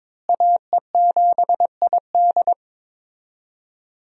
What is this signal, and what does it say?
Morse code, heard as a keyed tone, about 22 wpm, 700 Hz. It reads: AE7ID
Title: USA Call Signs - 5 Characters - Repeated 3x - 22wpm